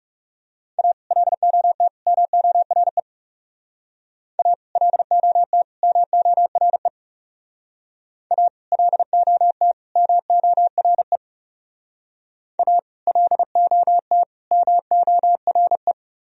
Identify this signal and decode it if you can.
Morse code, heard as a keyed tone, about 36 wpm, 700 Hz. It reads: A LOT MORE A LOT MORE A LOT MORE A LOT MORE
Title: Speed-Racing - Sets of 3 Words 30wpm